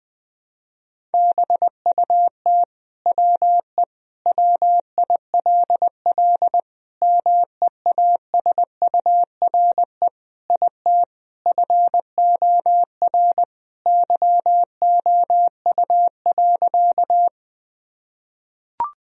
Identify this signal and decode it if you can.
Morse code, heard as a keyed tone, about 20 wpm, 700 Hz. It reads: BUT WE WILL MEASURE IT FOR YOU.
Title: Sentences from Top 1000 Words - Review Part 2 - 20wpm